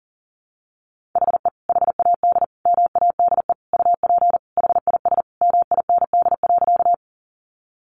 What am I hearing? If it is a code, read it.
Morse code, heard as a keyed tone, about 40 wpm, 700 Hz. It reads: HE HAD MADE UP HIS MIND.
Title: Sentences from Top 500 Words 40wpm